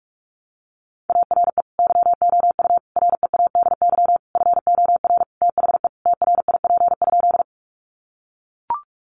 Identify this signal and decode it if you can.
Morse code, heard as a keyed tone, about 44 wpm, 700 Hz. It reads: ARE YOU READY FOR THE TRIP?
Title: Sentences from Top 1300 Words 45wpm